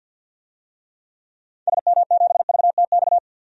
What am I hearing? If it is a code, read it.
Morse code, heard as a keyed tone, about 49 wpm, 700 Hz. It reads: SM7VTX
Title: DXCC-100 Call Signs - 6 Characters - 50wpm